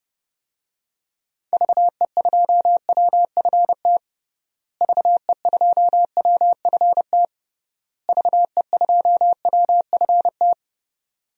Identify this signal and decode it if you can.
Morse code, heard as a keyed tone, about 32 wpm, 700 Hz. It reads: VE2WFT VE2WFT VE2WFT
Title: DXCC-100 Call Signs - 6 Characters - Repeated 3x - 30wpm